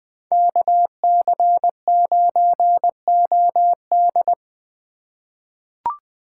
Morse code, heard as a keyed tone, about 20 wpm, 700 Hz. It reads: KC9OD
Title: USA Call Signs - 4x Character Spacing - 20wpm